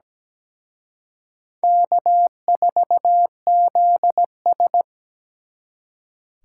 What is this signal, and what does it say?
Morse code, heard as a keyed tone, about 20 wpm, 700 Hz. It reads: K4ZS
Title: Speed Racing - USA 1x2 Call Signs 17wpm